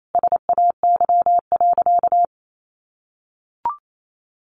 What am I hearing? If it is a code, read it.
Morse code, heard as a keyed tone, about 28 wpm, 700 Hz. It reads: SAY.
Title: Sentences from Top 1400 Words 28wpm